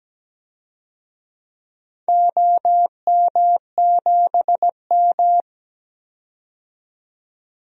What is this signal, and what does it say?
Morse code, heard as a keyed tone, about 17 wpm, 700 Hz. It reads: OM7M
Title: DXCC-100 Call Signs - 4 Characters - 17wpm